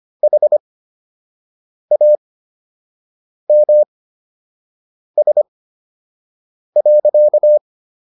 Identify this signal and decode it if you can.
Morse code, heard as a keyed tone, about 25 wpm, spaced as for 6 wpm, 600 Hz. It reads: HAMS.